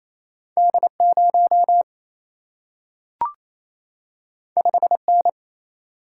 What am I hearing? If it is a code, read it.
Morse code, heard as a keyed tone, about 28 wpm, 700 Hz. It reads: D0 5N
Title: Single-Letter Single-Number 28wpm